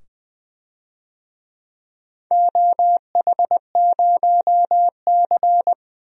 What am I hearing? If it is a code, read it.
Morse code, heard as a keyed tone, about 20 wpm, 700 Hz. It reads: OH0C
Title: DXCC-100 Call Signs - 4 Characters - 20wpm